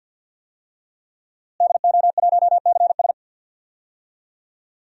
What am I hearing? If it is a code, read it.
Morse code, heard as a keyed tone, about 51 wpm, 700 Hz. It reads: DO1CS
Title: DXCC-100 Call Signs - 5 Characters - Repeated 3x - 50wpm